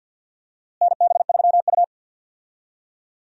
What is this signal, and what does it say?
Morse code, heard as a keyed tone, about 52 wpm, 700 Hz. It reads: NB3U
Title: USA Call Signs - 4 Characters - Repeated 2x - 50wpm